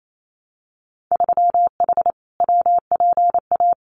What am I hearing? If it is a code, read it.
Morse code, heard as a keyed tone, about 28 wpm, 700 Hz. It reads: 3H WPA
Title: Winter Field Day Exchanges Encore 28wpm